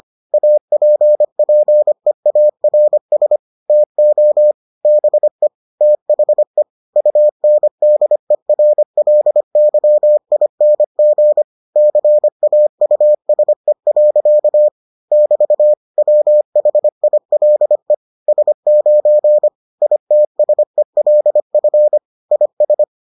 Morse code, heard as a keyed tone, about 25 wpm, 600 Hz. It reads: APPEARS TO BE THE UNDERLYING CAUSE. <BT> WHILE S9 ITSELF IS